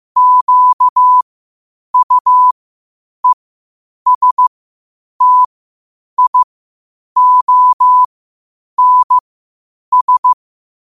Morse code, heard as a keyed tone, about 15 wpm, spaced as for 8 wpm, 1000 Hz. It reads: QUESTIONS
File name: qotdfiles / 8wpm